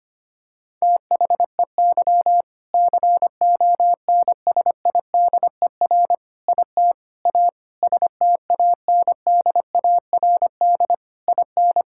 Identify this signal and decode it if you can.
Morse code, heard as a keyed tone, about 25 wpm, 700 Hz. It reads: THEY CONSIDER IT A STANDARD IN
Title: Sentences from Top 1000 Words - Review Part 6 - 25wpm